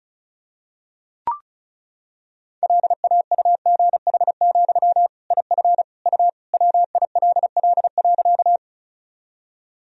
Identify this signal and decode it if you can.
Morse code, heard as a keyed tone, about 36 wpm, 700 Hz. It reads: LAUGH, IF U WILL.
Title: Sentences from Top 500 Words and Abbreviations 35wpm